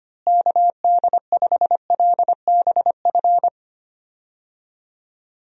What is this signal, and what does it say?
Morse code, heard as a keyed tone, about 25 wpm, 700 Hz. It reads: KD5LBF
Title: USA Call Signs - 6 Characters - Repeated 3x - 25wpm